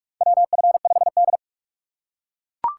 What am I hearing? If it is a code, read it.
Morse code, heard as a keyed tone, about 42 wpm, 700 Hz. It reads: WF5D
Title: USA Call Signs - 4 Characters - 2x Character Spacing - 45wpm